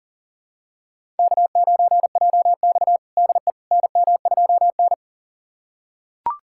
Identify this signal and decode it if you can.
Morse code, heard as a keyed tone, about 37 wpm, 700 Hz. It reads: K9JX DE NM2N
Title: USA Callsign DE Callsign - 4 Characters - 40wpm